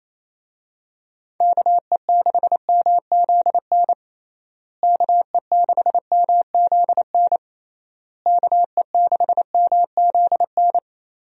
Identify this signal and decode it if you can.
Morse code, heard as a keyed tone, about 28 wpm, 700 Hz. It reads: KE6MZN KE6MZN KE6MZN
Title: USA Call Signs - 6 Characters - Repeated 3x - 28wpm